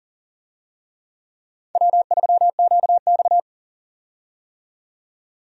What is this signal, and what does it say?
Morse code, heard as a keyed tone, about 40 wpm, 700 Hz. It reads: W3QX
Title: USA Call Signs - 4 Characters - 40wpm